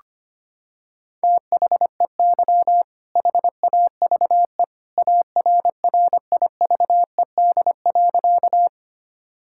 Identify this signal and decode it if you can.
Morse code, heard as a keyed tone, about 25 wpm, 700 Hz. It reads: THEY HAVE ARRIVED.
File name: practice / 25wpm / Sentences with 2-Word Phrases - 25wpm